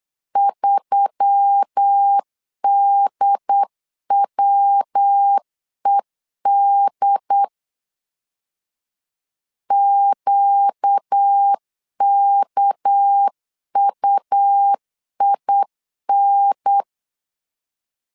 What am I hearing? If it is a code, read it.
Morse code, heard as a keyed tone, about 8 wpm, 800 Hz. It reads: 3DWED QKUIN
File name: CW / DET / 1 6 1500C S C 8wpm 5219